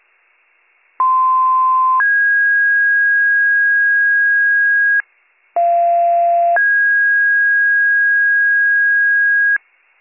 Two-tone page, heard as a one-second tone, then a three-second tone, about 1040 then 1650 Hz, about 690 then 1650 Hz. It is heard from 1.0 to 9.6 s.